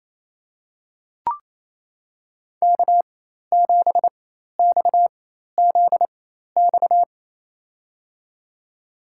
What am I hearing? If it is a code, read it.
Morse code, heard as a keyed tone, about 28 wpm, spaced as for 13 wpm, 700 Hz. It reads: K7XZX
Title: USA Call Signs - 5 Characters - 4x Character Spacing - 28wpm